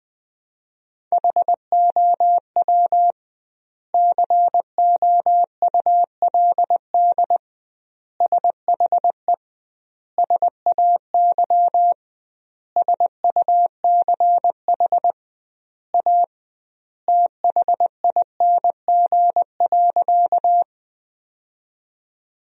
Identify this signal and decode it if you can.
Morse code, heard as a keyed tone, about 20 wpm, 700 Hz. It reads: HOW COULD SHE SAY SUCH A THING.
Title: 2x Word-Spacing - Sentences from Top 200 Words - 20wpm